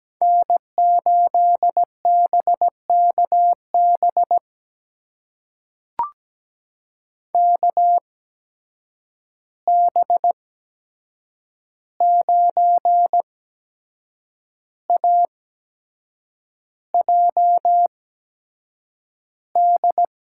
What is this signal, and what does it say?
Morse code, heard as a keyed tone, about 17 wpm, spaced as for 7 wpm, 700 Hz. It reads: N8BKB KB9AJD
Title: USA Call Signs - 8x Character Spacing - 17wpm